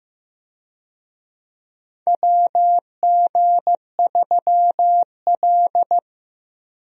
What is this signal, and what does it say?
Morse code, heard as a keyed tone, about 15 wpm, 700 Hz. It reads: WG3L
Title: USA Call Signs - 4 Characters - Repeated 3x - 15wpm